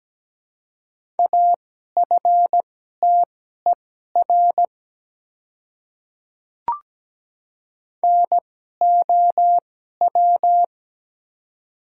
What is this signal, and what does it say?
Morse code, heard as a keyed tone, about 17 wpm, spaced as for 12 wpm, 700 Hz. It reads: AFTER NOW
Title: 2x Character Spacing - Top 200 Words - 17wpm